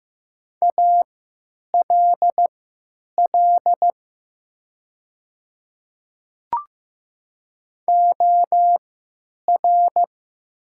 Morse code, heard as a keyed tone, about 15 wpm, spaced as for 9 wpm, 700 Hz. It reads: ALL OR
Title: 3x Character Spacing - Top 100 Words - 15wpm